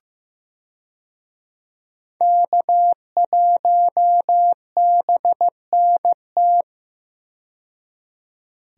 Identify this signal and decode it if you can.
Morse code, heard as a keyed tone, about 15 wpm, 700 Hz. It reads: K1BNT
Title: USA Call Signs - 5 Characters - 15wpm